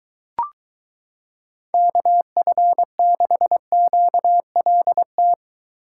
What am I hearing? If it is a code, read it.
Morse code, heard as a keyed tone, about 23 wpm, 700 Hz. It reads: KF6QLT